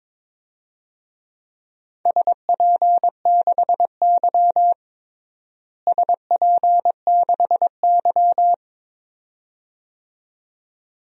Morse code, heard as a keyed tone, about 22 wpm, 700 Hz. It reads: SP6Y SP6Y
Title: DXCC-100 Call Signs - 4 Characters - Repeated 2x - 22wpm